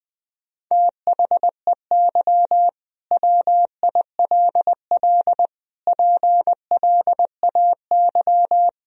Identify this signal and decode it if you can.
Morse code, heard as a keyed tone, about 20 wpm, 700 Hz. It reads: THEY WILL PLAY
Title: Sentences from Top 1000 Words - Review Part 4 - 20wpm